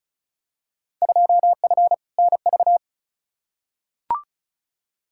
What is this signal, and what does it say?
Morse code, heard as a keyed tone, about 34 wpm, 700 Hz. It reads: 2F NV
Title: Field Day Exchanges - 35wpm